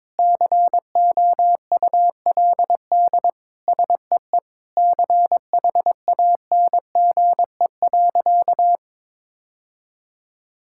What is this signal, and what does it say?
Morse code, heard as a keyed tone, about 22 wpm, 700 Hz. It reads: COULD SEE CHANGE.
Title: Sentences from Top 1000 Words - Review Part 5 - 22wpm